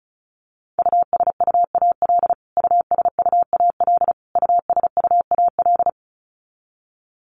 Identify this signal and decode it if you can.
Morse code, heard as a keyed tone, about 34 wpm, 700 Hz. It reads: USUAL USUAL USUAL
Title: Top 500 Words Repeated 35wpm